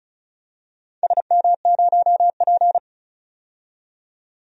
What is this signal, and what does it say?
Morse code, heard as a keyed tone, about 36 wpm, 700 Hz. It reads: SM0P
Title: DXCC-100 Call Signs - 4 Characters - 35wpm